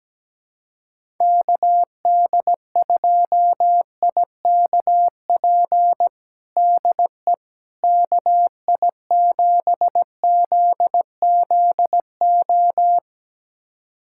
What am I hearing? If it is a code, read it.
Morse code, heard as a keyed tone, about 17 wpm, 700 Hz. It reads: KD2IKP DE KI7ZZO
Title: USA Callsign DE Callsign - 6 Characters - 17wpm